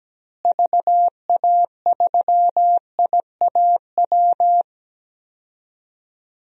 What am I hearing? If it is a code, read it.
Morse code, heard as a keyed tone, about 17 wpm, 700 Hz. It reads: VA3IAW
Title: DXCC-100 Call Signs - 6 Characters - Repeated 3x - 17wpm